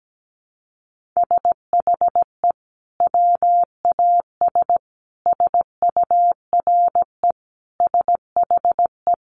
Morse code, heard as a keyed tone, about 17 wpm, 700 Hz. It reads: SHE WAS SURE SHE
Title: Sentences from Top 1500 Words 17wpm